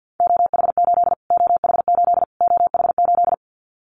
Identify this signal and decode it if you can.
Morse code, heard as a keyed tone, about 50 wpm, 700 Hz. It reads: OH8 OH8 OH8
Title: Callsign Prefixes for Top 25 DXCC - Repeated 3x - 50wpm